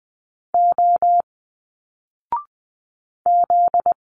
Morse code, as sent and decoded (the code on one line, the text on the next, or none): --- / --..
O Z